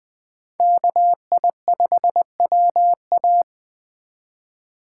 Morse code, heard as a keyed tone, about 20 wpm, 700 Hz. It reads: KI5WA
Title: 2023 SOTA Chaser Call Signs - 20wpm